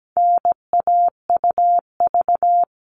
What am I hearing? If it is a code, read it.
Morse code, heard as a keyed tone, about 17 wpm, 700 Hz. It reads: NAUV
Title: All Cut Numbers - 4-Digit 17wpm